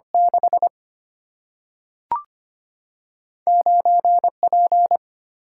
Morse code, heard as a keyed tone, about 25 wpm, 700 Hz. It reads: 6 9 P
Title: Single-Letter Single-Number 25wpm